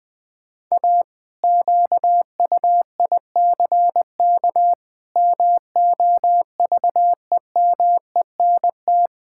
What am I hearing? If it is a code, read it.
Morse code, heard as a keyed tone, about 20 wpm, 700 Hz. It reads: A QUICK MOVEMENT
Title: Pangram Challenge v2 - 20wpm